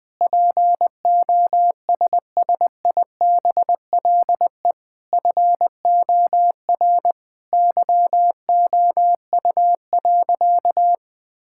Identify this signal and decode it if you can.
Morse code, heard as a keyed tone, about 21 wpm, 700 Hz. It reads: POSSIBLE FOR YOU.
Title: Sentences with 7-Word Phrases - 20wpm